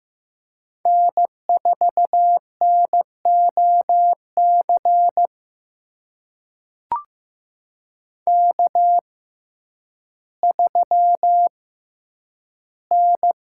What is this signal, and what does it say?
Morse code, heard as a keyed tone, about 15 wpm, spaced as for 9 wpm, 700 Hz. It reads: N4NOC K3N
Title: USA Call Signs - 5 Characters - 6x Character Spacing - 15wpm